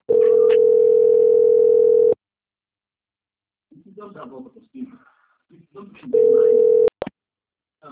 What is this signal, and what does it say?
Ringback tone (North America), 440 + 480 Hz, with speech.
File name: out- - - - - .30625